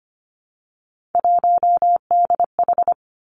Morse code, heard as a keyed tone, about 25 wpm, 700 Hz. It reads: 1DH
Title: Three Challenging Letter-Number Combinations 25wpm